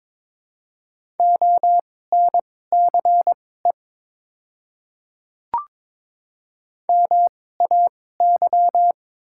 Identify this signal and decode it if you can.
Morse code, heard as a keyed tone, about 22 wpm, spaced as for 16 wpm, 700 Hz. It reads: ONCE MAY
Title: 2x Character Spacing - Top 300 Words - 22wpm